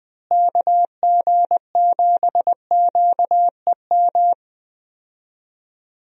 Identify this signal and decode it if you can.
Morse code, heard as a keyed tone, about 20 wpm, 700 Hz. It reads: KG7QEM
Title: USA Call Signs - 6 Characters - Repeated 3x - 20wpm